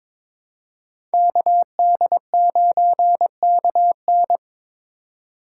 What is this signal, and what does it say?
Morse code, heard as a keyed tone, about 26 wpm, 700 Hz. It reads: KD9KN
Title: Speed Racing - USA 2x2 Call Signs 22wpm